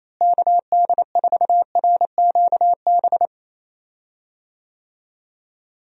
Morse code, heard as a keyed tone, about 27 wpm, 700 Hz. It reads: KD4RQB
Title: USA Call Signs - 6 Characters - Repeated 3x - 28wpm